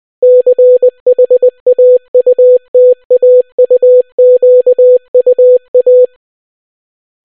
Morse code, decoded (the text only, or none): CHAUTAUQUA